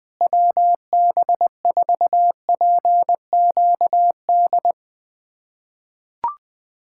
Morse code, heard as a keyed tone, about 21 wpm, 700 Hz. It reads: WB4PQD
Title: USA Callsign DE Callsign - 6 Characters - 20wpm